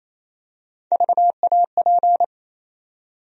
Morse code, heard as a keyed tone, about 28 wpm, 700 Hz. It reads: VAP
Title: USA Call Sign Postfixes - 3 Letters 28wpm